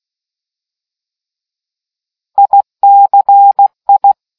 Morse code, heard as a keyed tone, about 16 wpm, 800 Hz. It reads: ICI